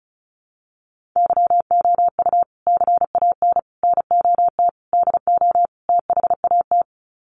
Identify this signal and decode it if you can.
Morse code, heard as a keyed tone, about 34 wpm, 700 Hz. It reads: YOU CAN NOT DO THAT
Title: Sets of 5 Words 35wpm